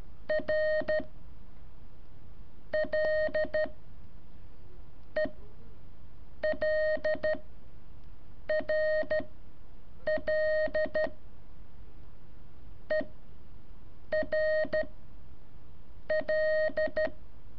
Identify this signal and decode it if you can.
Morse code, heard as a keyed tone, about 13 wpm, 620 Hz. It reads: R L E L R L E R L
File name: curso / ejercicio 2